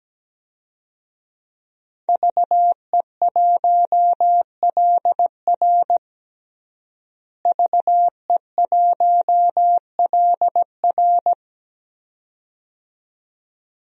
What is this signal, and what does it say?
Morse code, heard as a keyed tone, about 17 wpm, 700 Hz. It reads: VE1LR VE1LR
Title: DXCC-100 Call Signs - 5 Characters - Repeated 2x - 17wpm